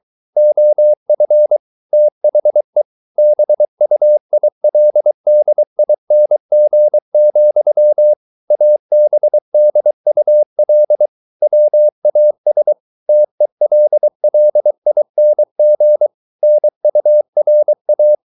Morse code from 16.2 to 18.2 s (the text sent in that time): NURA